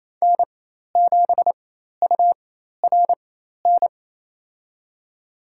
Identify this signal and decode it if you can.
Morse code, heard as a keyed tone, about 28 wpm, spaced as for 13 wpm, 700 Hz. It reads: N7URN
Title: USA Call Signs - 5 Characters - 4x Character Spacing - 28wpm